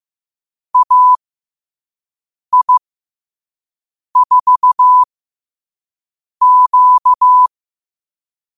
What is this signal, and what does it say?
Morse code, heard as a keyed tone, about 15 wpm, spaced as for 5 wpm, 1000 Hz. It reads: AI4Q
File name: qotdfiles / 5wpm